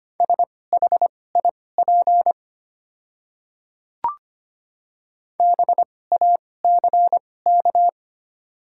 Morse code, heard as a keyed tone, about 25 wpm, spaced as for 18 wpm, 690 Hz. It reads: SHIP BACK
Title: 2x Character Spacing - Top 500 Words - 25wpm